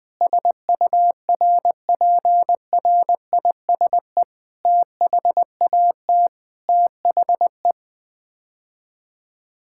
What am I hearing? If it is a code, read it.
Morse code, heard as a keyed tone, about 21 wpm, 700 Hz. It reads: SURPRISE THAT THE
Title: Sets of 9 Words - 20wpm